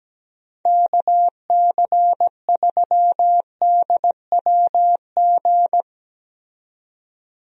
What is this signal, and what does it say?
Morse code, heard as a keyed tone, about 17 wpm, 700 Hz. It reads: KC3DWG
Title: USA Call Signs - 6 Characters - 17wpm